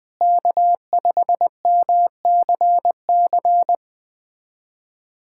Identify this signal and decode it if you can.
Morse code, heard as a keyed tone, about 21 wpm, 700 Hz. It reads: K5MCC
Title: USA Callsign DE Callsign - 5 Characters - 20wpm